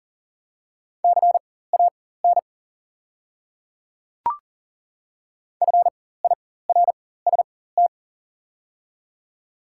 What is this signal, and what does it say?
Morse code, heard as a keyed tone, about 40 wpm, spaced as for 19 wpm, 700 Hz. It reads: CAN FIRST
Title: 4x Character Spacing - Top 100 Words - 40wpm